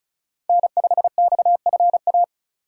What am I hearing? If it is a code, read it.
Morse code, heard as a keyed tone, about 44 wpm, 700 Hz. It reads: N5XFA
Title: Speed Racing - USA Call Signs 35wpm